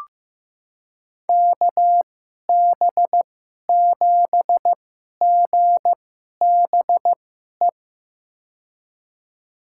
Morse code, heard as a keyed tone, about 15 wpm, spaced as for 11 wpm, 700 Hz. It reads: KB7GBE
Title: USA Call Signs - 2x Character Spacing - 15wpm